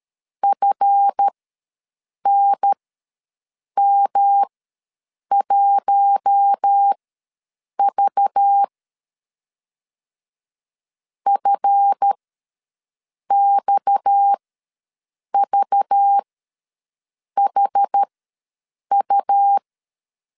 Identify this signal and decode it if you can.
Morse code, heard as a keyed tone, about 13 wpm, spaced as for 6 wpm, 800 Hz. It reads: FNM1V FXVHU
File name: LEZIONI / 7 GF9Z / 7 GF ws 12wpm 3140